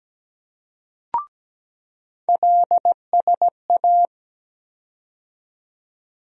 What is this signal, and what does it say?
Morse code, heard as a keyed tone, about 17 wpm, 700 Hz. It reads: LSA
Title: USA Call Sign Postfixes - 3 Letters 17wpm